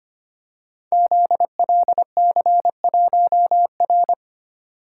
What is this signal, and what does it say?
Morse code, heard as a keyed tone, about 25 wpm, 700 Hz. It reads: ZLC1R
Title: Code Groups - 5 Letter:Numbers - 25wpm